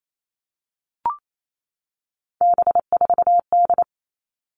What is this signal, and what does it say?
Morse code, heard as a keyed tone, about 28 wpm, 700 Hz. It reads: B4D
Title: All Cut Numbers - 3-Digit 28wpm